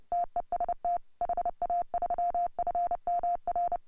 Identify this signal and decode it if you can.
Morse code, heard as a keyed tone, about 30 wpm, 690 Hz. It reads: TEST HA3FMR
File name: AUDIO2018 / HA3FMR CW